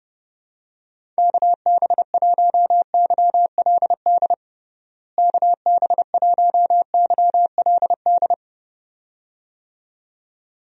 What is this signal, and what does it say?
Morse code, heard as a keyed tone, about 32 wpm, 700 Hz. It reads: KB1YLD KB1YLD
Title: USA Call Signs - 6 Characters - Repeated 2x - 30wpm